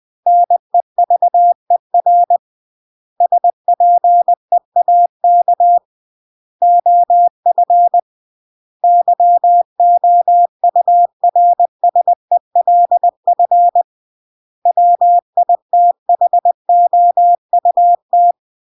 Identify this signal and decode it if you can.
Morse code, heard as a keyed tone, about 20 wpm, 700 Hz. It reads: NEVER SPEAK OF YOURSELF WITHOUT
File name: CW / jodorowsky / 82 commandments 20-20